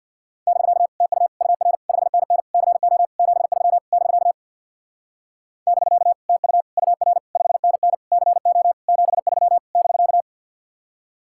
Morse code, heard as a keyed tone, about 64 wpm, 700 Hz. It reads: <BK> TU UR 5NN CO 73 <BK> <BK> TU UR 5NN CO 73 <BK>